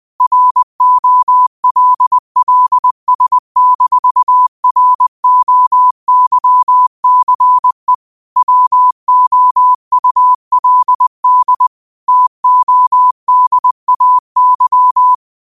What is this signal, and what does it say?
Morse code, heard as a keyed tone, about 20 wpm, 1000 Hz. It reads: ROLLS-ROYCE WOULD TODAY
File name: qotdfiles / 20wpm